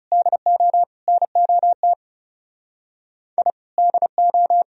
Morse code, heard as a keyed tone, about 37 wpm, 700 Hz. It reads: DO NOT I DO
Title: Speed-Racing - Sets of 3 Words 30wpm